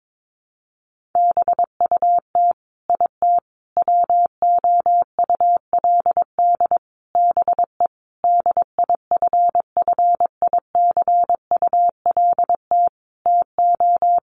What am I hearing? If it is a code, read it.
Morse code, heard as a keyed tone, about 22 wpm, 700 Hz. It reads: BUT IT WOULD BE DIFFICULT TO
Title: Sentences from Top 1400 Words 22wpm